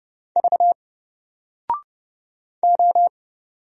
Morse code, heard as a keyed tone, about 29 wpm, 700 Hz. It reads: V O
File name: Single-Letter-Number-30wpm